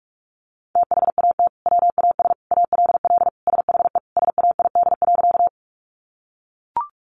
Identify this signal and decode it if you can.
Morse code, heard as a keyed tone, about 44 wpm, 700 Hz. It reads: THAT WAS ALL SHE SAID.
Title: Sentences from Top 100 Words 45wpm